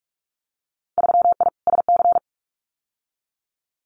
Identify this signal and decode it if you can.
Morse code, heard as a keyed tone, about 46 wpm, 700 Hz. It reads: 3I SC
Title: Winter Field Day Exchanges Encore 45wpm